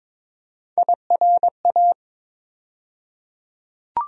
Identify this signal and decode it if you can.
Morse code, heard as a keyed tone, about 22 wpm, 700 Hz. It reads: IRA
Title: USA Call Sign Postfixes - 3 Letters 22wpm